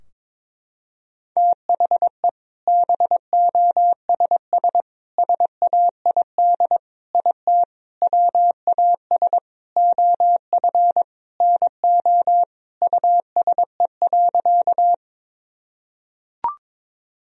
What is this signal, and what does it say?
Morse code, heard as a keyed tone, about 22 wpm, 700 Hz. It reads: THE BOSS SAID IT WAS OF NO USE.
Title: Sentences from Top 1000 Words - Review Part 4 - 22wpm